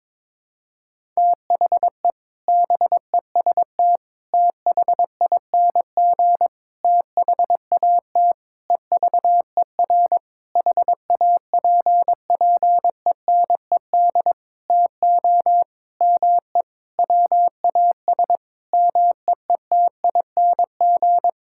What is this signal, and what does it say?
Morse code, heard as a keyed tone, about 21 wpm, 700 Hz. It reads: THE BEST THING THAT EVER HAPPENED TO ME WAS MEETING
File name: practice / 22wpm / Sentences with 7-Word Phrases - 22wpm